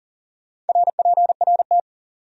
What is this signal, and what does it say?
Morse code, heard as a keyed tone, about 52 wpm, 700 Hz. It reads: RPRT